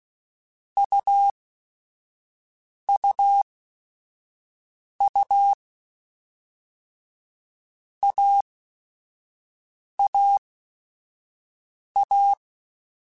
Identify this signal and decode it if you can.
Morse code, heard as a keyed tone, about 16 wpm, spaced as for 5 wpm, 800 Hz. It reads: UUU AAA